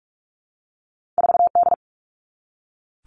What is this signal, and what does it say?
Morse code, heard as a keyed tone, about 46 wpm, 700 Hz. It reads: 4D
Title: All Cut Numbers - 2-Digit 45wpm